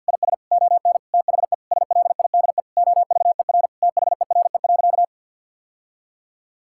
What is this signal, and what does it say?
Morse code, heard as a keyed tone, about 50 wpm, 700 Hz. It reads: IS ON THE SLIDE OVER THERE.
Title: Sentences from Top 1600 Words 50wpm